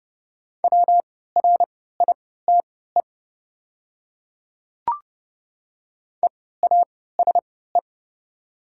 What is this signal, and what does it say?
Morse code, heard as a keyed tone, about 29 wpm, spaced as for 17 wpm, 700 Hz. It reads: WRITE EASE
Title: 3x Character Spacing - Top 500 Words - 30wpm